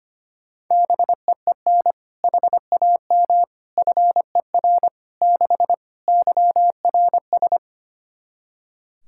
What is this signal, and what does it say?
Morse code, heard as a keyed tone, about 25 wpm, 700 Hz. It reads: BEEN HAM FER 6 YRS
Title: K1USN SST QSO Practice 25wpm